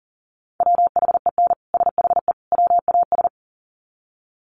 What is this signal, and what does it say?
Morse code, heard as a keyed tone, about 40 wpm, 700 Hz. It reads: WHEN SHE WAS
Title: Sets of 3 Words 40wpm